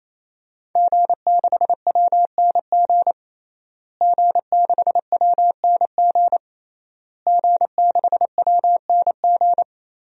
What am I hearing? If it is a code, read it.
Morse code, heard as a keyed tone, about 28 wpm, 700 Hz. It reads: G6WNG G6WNG G6WNG
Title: DXCC-100 Call Signs - 5 Characters - Repeated 3x - 28wpm